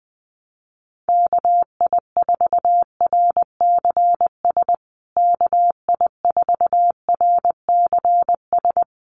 Morse code, heard as a keyed tone, about 20 wpm, 700 Hz. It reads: KI4RCS KI4RCS
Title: USA Call Signs Repeated 2x 20wpm